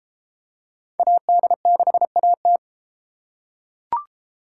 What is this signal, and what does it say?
Morse code, heard as a keyed tone, about 33 wpm, 700 Hz. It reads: AD6AT